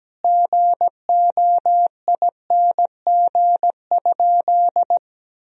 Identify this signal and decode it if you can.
Morse code, heard as a keyed tone, about 17 wpm, 700 Hz. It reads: GOING?
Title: Sentences from Top 1000 Words - Review Part 3 - 17wpm